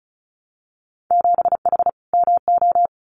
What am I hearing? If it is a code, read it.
Morse code, heard as a keyed tone, about 36 wpm, 700 Hz. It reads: 7H MO
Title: Winter Field Day Exchanges Encore 35wpm